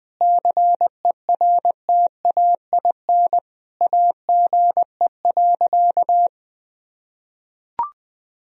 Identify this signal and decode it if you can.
Morse code, heard as a keyed tone, about 20 wpm, 700 Hz. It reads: CERTAIN AGE.
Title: Sentences from Top 1000 Words - Review Part 3 - 20wpm